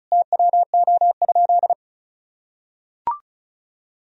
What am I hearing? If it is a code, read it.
Morse code, heard as a keyed tone, about 36 wpm, 700 Hz. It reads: TWO?
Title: Sentences from Top 1000 Words - Review Part 2 - 35wpm